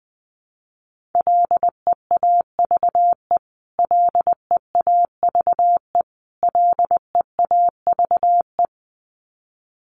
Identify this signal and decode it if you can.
Morse code, heard as a keyed tone, about 20 wpm, 700 Hz. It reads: LEAVE LEAVE LEAVE
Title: Top 500 Words Repeated 20wpm